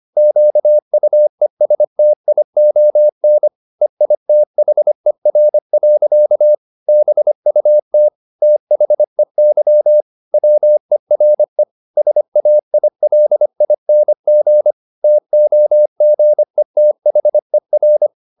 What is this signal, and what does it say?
Morse code, heard as a keyed tone, about 25 wpm, 600 Hz. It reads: QUESTION EITHER. BUT THEY WERE SAILING TOGETHER